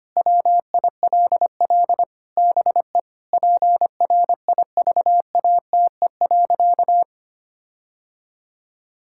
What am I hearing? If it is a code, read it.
Morse code, heard as a keyed tone, about 25 wpm, 700 Hz. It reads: WILL BE PRIVATE.
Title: Sentences from Top 1600 Words 25wpm